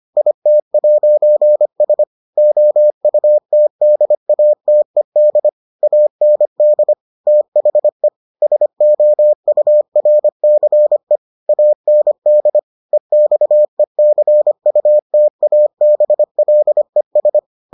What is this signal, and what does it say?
Morse code, heard as a keyed tone, about 25 wpm, 600 Hz. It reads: IT'S OUTDATED AND THE SOURCE AND EXECUTABLES